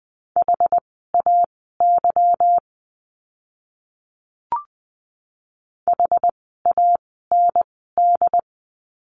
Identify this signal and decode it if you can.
Morse code, heard as a keyed tone, about 20 wpm, spaced as for 14 wpm, 700 Hz. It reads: HAY HAND
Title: 146 Lesson 24f - Letter y - Words 20wpm